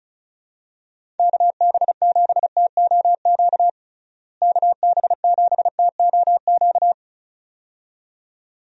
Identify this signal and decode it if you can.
Morse code, heard as a keyed tone, about 34 wpm, 700 Hz. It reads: KB7TOQ KB7TOQ
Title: USA Call Signs - 6 Characters - Repeated 2x - 35wpm